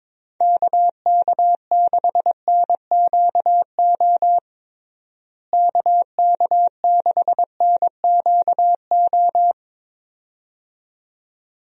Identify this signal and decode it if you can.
Morse code, heard as a keyed tone, about 22 wpm, 700 Hz. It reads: KK6NQO KK6NQO
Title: USA Call Signs - 6 Characters - Repeated 2x - 22wpm